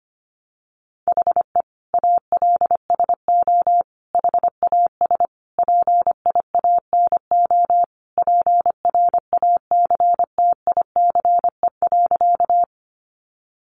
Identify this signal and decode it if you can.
Morse code, heard as a keyed tone, about 25 wpm, 700 Hz. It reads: HE ALSO HAS PIANO PRACTICE.
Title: Additional Easy Phrases 25wpm